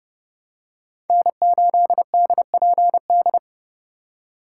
Morse code, heard as a keyed tone, about 30 wpm, 700 Hz. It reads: N8DPD
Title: K1USN SST QSO Practice - 30wpm